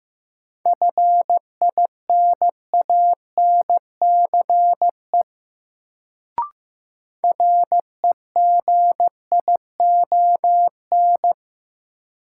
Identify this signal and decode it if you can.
Morse code, heard as a keyed tone, about 15 wpm, 700 Hz. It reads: FINANCE REGION
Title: Sentences from Top 1500 Words - New Word Drill - 15wpm